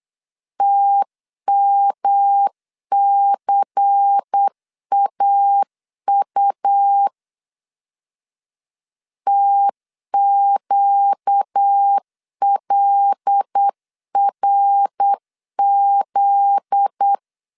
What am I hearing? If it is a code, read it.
Morse code, heard as a keyed tone, about 8 wpm, 800 Hz. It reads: TMCAU TQLRZ
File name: LEZIONI / 8 YJ45 / 8 Y 8wpm 3245